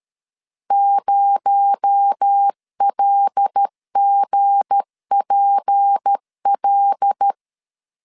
Morse code, heard as a keyed tone, about 13 wpm, 800 Hz. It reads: 0LGPL
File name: LEZIONI / 7 GF9Z / 7 GF9Z 12wpm 2147